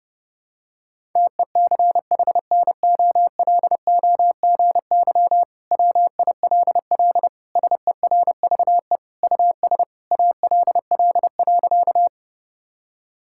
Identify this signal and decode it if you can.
Morse code, heard as a keyed tone, about 30 wpm, 700 Hz. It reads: TECHNOLOGY WILL SERVE US ALL.
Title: Sentences from Top 1000 Words - Review Part 5 - 30wpm